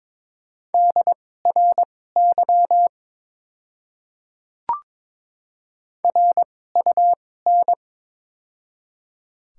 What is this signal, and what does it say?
Morse code, heard as a keyed tone, about 22 wpm, spaced as for 16 wpm, 700 Hz. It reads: DRY RUN
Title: 2x Character Spacing - Top 500 Words - 22wpm